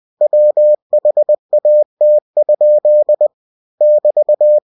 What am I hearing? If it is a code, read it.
Morse code, heard as a keyed tone, about 20 wpm, 600 Hz. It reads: WHAT? <BT>
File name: winnie-20wpm